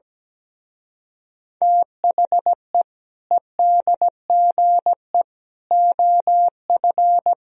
Morse code, heard as a keyed tone, about 17 wpm, 700 Hz. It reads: THE EDGE OF
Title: Speed-Racing - Sets of 3 Words 17wpm